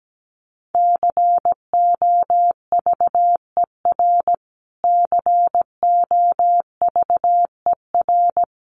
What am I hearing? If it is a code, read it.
Morse code, heard as a keyed tone, about 17 wpm, 700 Hz. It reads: COVER COVER
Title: Top 500 Words Repeated 17wpm